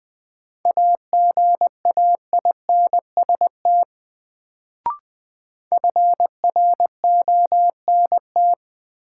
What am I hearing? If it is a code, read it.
Morse code, heard as a keyed tone, about 20 wpm, 700 Hz. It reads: AGAINST FRONT
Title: Sets of 2 Words - New Word Drill - 20wpm